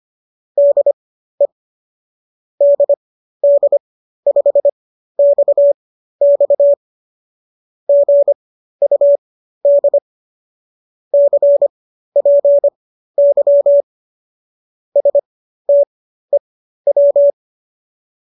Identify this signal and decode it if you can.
Morse code, heard as a keyed tone, about 25 wpm, spaced as for 13 wpm, 570 Hz. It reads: DE DD5XX GUD CPY STEW